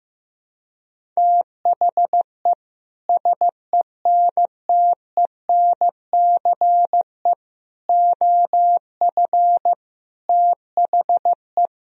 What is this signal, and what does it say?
Morse code, heard as a keyed tone, about 15 wpm, 700 Hz. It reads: THE SENTENCE OF THE
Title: Sentences from Top 1000 Words - Review Part 1 - 15wpm